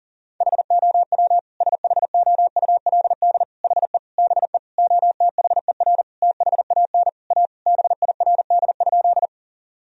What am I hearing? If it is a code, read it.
Morse code, heard as a keyed tone, about 40 wpm, 700 Hz. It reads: HOW SHOULD HE BE OTHER THAN A BIRD?
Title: Sentences from Top 1000 Words - Review Part 2 - 40wpm